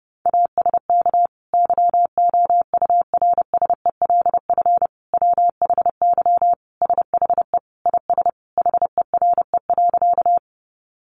Morse code, heard as a keyed tone, about 30 wpm, 700 Hz. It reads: ASK YOURSELF WHY SHE IS HERE.
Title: Sentences from Top 600 Words 30wpm